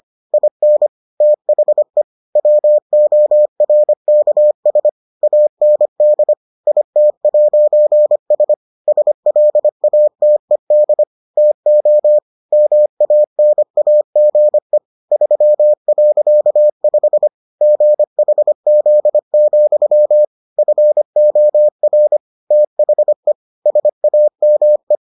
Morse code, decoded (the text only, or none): IN THE WORKS AND IT'S SLATED TO MANAGE 3.5 GHZ, FOR THE SAME